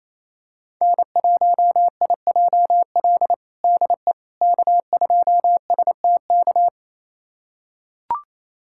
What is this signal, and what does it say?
Morse code, heard as a keyed tone, about 28 wpm, 700 Hz. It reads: N1IJL DE K2STK
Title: USA Callsign DE Callsign - 5 Characters - 28wpm